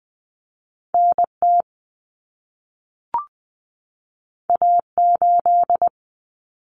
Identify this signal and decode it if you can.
Morse code, heard as a keyed tone, about 20 wpm, 700 Hz. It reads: NT A8
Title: Common Cut Numbers - 2-Digit 20wpm